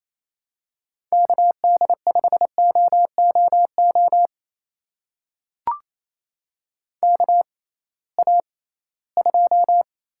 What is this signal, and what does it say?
Morse code, heard as a keyed tone, about 28 wpm, spaced as for 18 wpm, 700 Hz. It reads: KD5OOO KA2
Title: USA Call Signs - 6 Characters - 6x Character Spacing - 28wpm